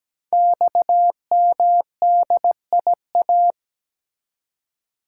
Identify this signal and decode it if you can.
Morse code, heard as a keyed tone, about 17 wpm, 700 Hz. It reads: XMDIA
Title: Code Groups - 5 Letters - 17wpm